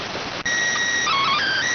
SSTV (slow-scan television), PD 290.